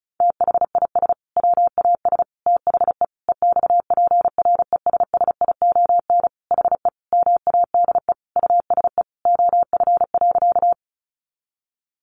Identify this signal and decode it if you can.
Morse code, heard as a keyed tone, about 34 wpm, 700 Hz. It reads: THIS WAS THE EXPRESSION HE MADE USE OF.
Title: Sentences from Top 1200 Words 35wpm